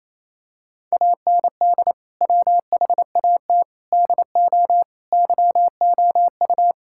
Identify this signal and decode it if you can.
Morse code, heard as a keyed tone, about 28 wpm, 700 Hz. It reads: AND WHAT DO YOU
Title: Sentences from Top 1600 Words 28wpm